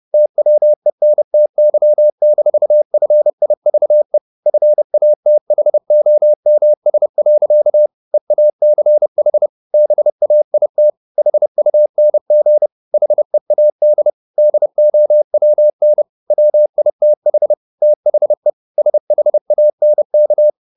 Morse code, heard as a keyed tone, about 30 wpm, 600 Hz. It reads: TWENTY-FIVE FATHOMS. EACH BAIT HUNG HEAD DOWN WITH THE SHANK
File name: the old man and the sea / the old man and the sea 1 30wpm